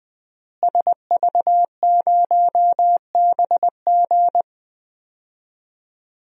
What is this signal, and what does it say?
Morse code, heard as a keyed tone, about 20 wpm, 700 Hz. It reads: SV0BG
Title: DXCC-100 Call Signs - 5 Characters - Repeated 3x - 20wpm